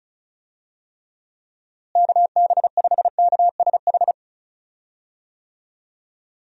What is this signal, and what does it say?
Morse code, heard as a keyed tone, about 34 wpm, 700 Hz. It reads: KB5KSH
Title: USA Call Signs - 6 Characters - Repeated 2x - 35wpm